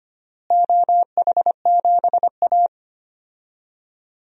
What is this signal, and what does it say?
Morse code, heard as a keyed tone, about 25 wpm, 700 Hz. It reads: OH7A
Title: DXCC-100 Call Signs - 4 Characters - Repeated 2x - 25wpm